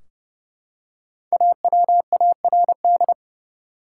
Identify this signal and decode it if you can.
Morse code, heard as a keyed tone, about 32 wpm, 700 Hz. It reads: AWARD
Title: Five Letter Words - 30wpm